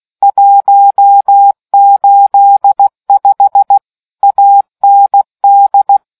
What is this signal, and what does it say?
Morse code, heard as a keyed tone, about 16 wpm, 800 Hz. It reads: 185 AND